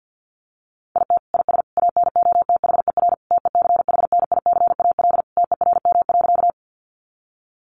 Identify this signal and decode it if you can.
Morse code, heard as a keyed tone, about 49 wpm, 700 Hz. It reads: IT IS ANOTHER TECHNICAL TERM.
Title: Sentences from Top 1100 Words 50wpm